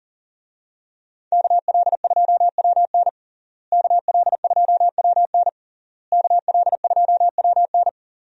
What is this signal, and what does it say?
Morse code, heard as a keyed tone, about 40 wpm, 700 Hz. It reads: KL2WN KL2WN KL2WN
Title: USA Call Signs - 5 Characters - Repeated 3x - 40wpm